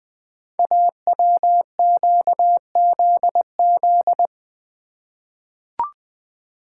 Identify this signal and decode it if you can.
Morse code, heard as a keyed tone, about 21 wpm, 700 Hz. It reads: AWQZZ
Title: Code Groups - 5 Letters - 20wpm